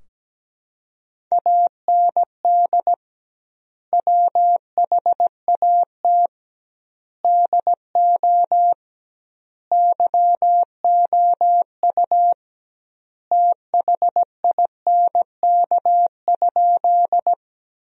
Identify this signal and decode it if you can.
Morse code, heard as a keyed tone, about 17 wpm, 700 Hz. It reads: AND WHAT DO YOU THINK?
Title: 2x Word-Spacing - Sentences from Top 200 Words - 17wpm